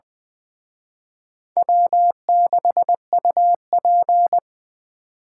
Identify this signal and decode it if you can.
Morse code, heard as a keyed tone, about 23 wpm, 700 Hz. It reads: W6UP
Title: Speed Racing - USA 1x2 Call Signs 20wpm